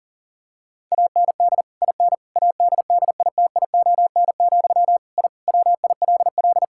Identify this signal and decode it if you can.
Morse code, heard as a keyed tone, about 40 wpm, 700 Hz. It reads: AND IN ADDITION, I WILL
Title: Sentences from Top 1000 Words - Review Part 6 - 40wpm